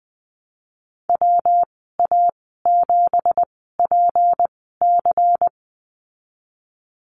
Morse code, heard as a keyed tone, about 19 wpm, spaced as for 14 wpm, 700 Hz. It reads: WA7PC
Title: 186 Lesson 30g - Letter b - Call Signs 20wpm